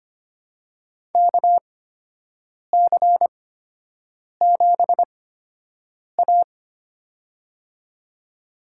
Morse code, heard as a keyed tone, about 25 wpm, spaced as for 7 wpm, 700 Hz. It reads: KC7A
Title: USA Call Signs - 4 Characters - 8x Character Spacing - 25wpm